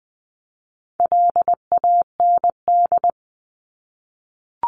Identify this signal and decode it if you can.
Morse code, heard as a keyed tone, about 20 wpm, 700 Hz. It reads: LAND
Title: Top 500 Words 20wpm